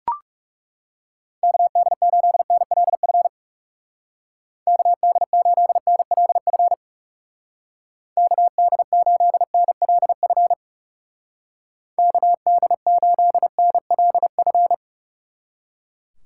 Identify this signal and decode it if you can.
Morse code, heard as a keyed tone, about 39 wpm, 700 Hz. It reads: KD8NLF KD8NLF KD8NLF KD8NLF